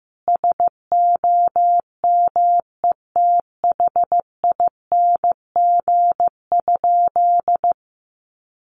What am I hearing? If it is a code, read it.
Morse code, heard as a keyed tone, about 15 wpm, 700 Hz. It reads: SOMETHING?
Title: Sentences from Top 1100 Words 15wpm